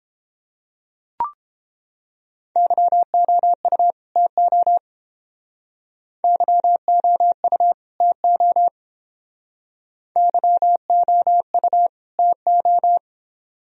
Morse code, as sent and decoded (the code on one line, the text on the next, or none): -.-- --- ..- / - --- / -.-- --- ..- / - --- / -.-- --- ..- / - ---
YOU TO YOU TO YOU TO